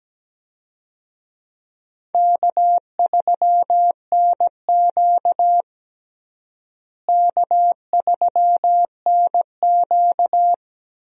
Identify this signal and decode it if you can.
Morse code, heard as a keyed tone, about 17 wpm, 700 Hz. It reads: K3NQ K3NQ
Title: USA Call Signs - 4 Characters - Repeated 2x - 17wpm